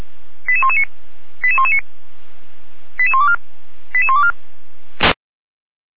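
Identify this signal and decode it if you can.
Five-tone selective call (ZVEI), heard as quick runs of five tones, tones of about 70 ms. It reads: 88119 88119 88124 88124
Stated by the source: Five-tone ZVEI selective call received with an SDR, Feuerwehr Orlamünde.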